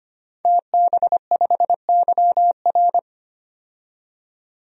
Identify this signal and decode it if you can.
Morse code, heard as a keyed tone, about 25 wpm, 700 Hz. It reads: TB5YR
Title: DXCC-100 Call Signs - 5 Characters - 25wpm